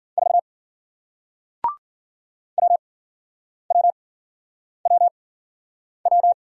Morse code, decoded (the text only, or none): 4 WWWW